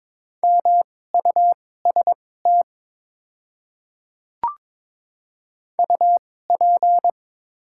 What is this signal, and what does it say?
Morse code, heard as a keyed tone, about 22 wpm, spaced as for 16 wpm, 700 Hz. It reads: MUST UP
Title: 2x Character Spacing - Top 500 Words - 22wpm